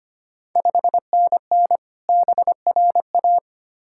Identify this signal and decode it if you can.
Morse code, heard as a keyed tone, about 25 wpm, 690 Hz. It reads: 5NN BRA
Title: Ontario QSO Party Exchange 25wpm